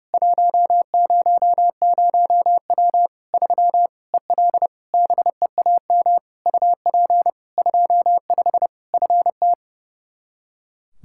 Morse code, heard as a keyed tone, about 32 wpm, 700 Hz. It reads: 100W 3 EL BEAM UP 25 FT
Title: Practice QSO - Exchange 2 - Info - 30wpm